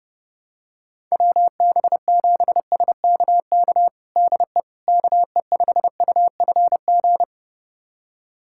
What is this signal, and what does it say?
Morse code, heard as a keyed tone, about 30 wpm, 700 Hz. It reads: WB7SKK DE KE5UFG
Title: USA Callsign DE Callsign - 6 Characters - 30wpm